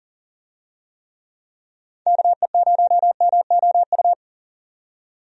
Morse code, heard as a keyed tone, about 38 wpm, 700 Hz. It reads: KE0MOU
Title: USA Call Signs - 6 Characters - 40wpm